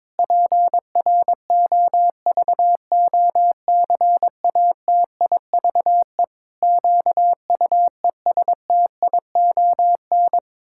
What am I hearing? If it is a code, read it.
Morse code, heard as a keyed tone, about 22 wpm, 700 Hz. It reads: PROVOCATIVE QUESTION
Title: Pangram Challenge v2 - 22wpm